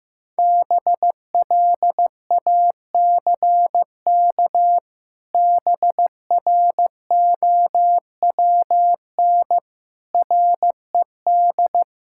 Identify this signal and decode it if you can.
Morse code, heard as a keyed tone, about 15 wpm, 700 Hz. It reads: BLACK BROWN RED
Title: Pangram Challenge v2 - 15wpm